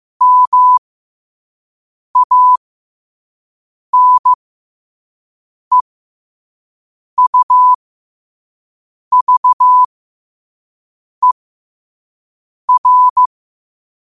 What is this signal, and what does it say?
Morse code, heard as a keyed tone, about 15 wpm, spaced as for 5 wpm, 1000 Hz. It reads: MANEUVER
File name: qotdfiles / 5wpm